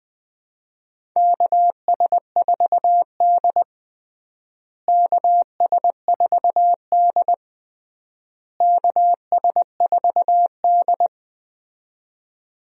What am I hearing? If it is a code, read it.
Morse code, heard as a keyed tone, about 20 wpm, 700 Hz. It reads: KS4D KS4D KS4D
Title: USA Call Signs - 4 Characters - Repeated 3x - 20wpm